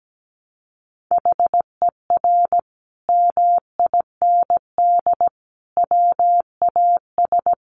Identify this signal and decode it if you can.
Morse code, heard as a keyed tone, about 17 wpm, 700 Hz. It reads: HER MIND WAS
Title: Sentences from Top 1400 Words 17wpm